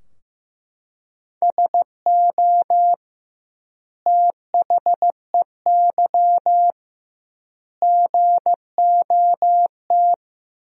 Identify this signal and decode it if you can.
Morse code, heard as a keyed tone, about 15 wpm, 700 Hz. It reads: SO THEY GOT
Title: 2x Word-Spacing - Sentences from Top 300 Words - 15wpm